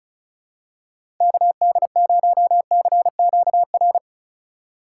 Morse code, heard as a keyed tone, about 36 wpm, 700 Hz. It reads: KD0CQR
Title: USA Call Signs - 6 Characters - 2x Character Spacing - 35wpm